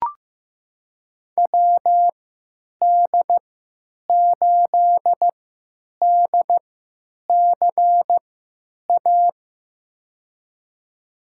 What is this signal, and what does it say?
Morse code, heard as a keyed tone, about 15 wpm, spaced as for 9 wpm, 700 Hz. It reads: WD8DCA
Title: USA Call Signs - 3x Character Spacing - 15wpm